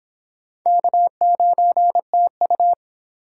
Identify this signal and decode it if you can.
Morse code, heard as a keyed tone, about 26 wpm, 700 Hz. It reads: K9TU